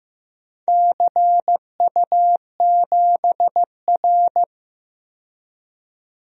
Morse code, heard as a keyed tone, about 15 wpm, 700 Hz. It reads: CU7R
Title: DXCC-100 Call Signs - 4 Characters - 15wpm